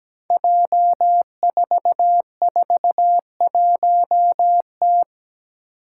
Morse code, heard as a keyed tone, about 17 wpm, 700 Hz. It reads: J441T
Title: Five Challenging Letter-Number Combinations 17wpm